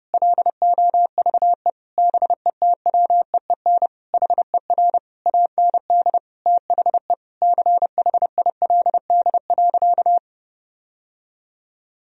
Morse code, heard as a keyed tone, about 30 wpm, 700 Hz. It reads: LOVE BETWEEN HER AND THE CHILD.
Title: Sentences from Top 1000 Words - Review Part 1 - 30wpm